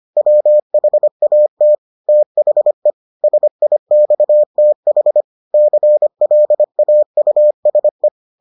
Morse code, heard as a keyed tone, about 25 wpm, 600 Hz. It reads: WHAT THE SIXTH CLAUSE